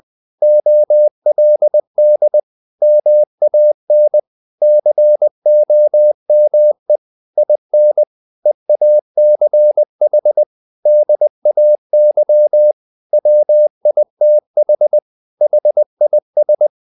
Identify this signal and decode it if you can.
Morse code, heard as a keyed tone, about 20 wpm, 600 Hz. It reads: OLD MAN COME IN EACH DAY WITH HIS
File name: the old man and the sea / the old man and the sea part 1 20wpm